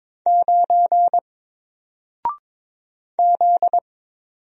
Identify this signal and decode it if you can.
Morse code, heard as a keyed tone, about 22 wpm, 700 Hz. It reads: 9 Z
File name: Single-Letter-Number-22wpm